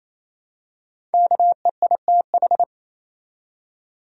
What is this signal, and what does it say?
Morse code, heard as a keyed tone, about 28 wpm, 700 Hz. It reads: KEITH
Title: USA Common Names - Max 5 Letters - 28wpm